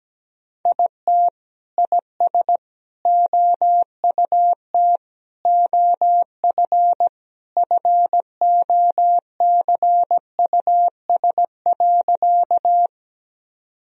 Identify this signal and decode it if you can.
Morse code, heard as a keyed tone, about 17 wpm, 700 Hz. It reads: IT IS OUT OF FOCUS.
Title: Sentences from Top 1000 Words - Review Part 4 - 17wpm